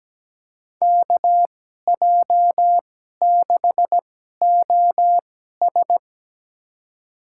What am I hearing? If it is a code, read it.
Morse code, heard as a keyed tone, about 17 wpm, spaced as for 12 wpm, 700 Hz. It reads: KJ6OS
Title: USA Call Signs - 2x Character Spacing - 17wpm